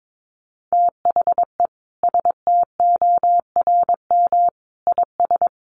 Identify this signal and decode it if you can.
Morse code, heard as a keyed tone, about 22 wpm, 700 Hz. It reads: THE STORM IS
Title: Sentences from Top 1500 Words 22wpm